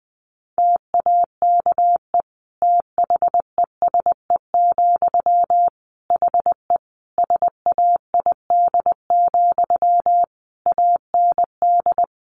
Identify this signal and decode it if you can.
Morse code, heard as a keyed tone, about 20 wpm, 700 Hz. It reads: TAKE THESE, HE SAID, AND
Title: Sentences from Top 300 Words 20wpm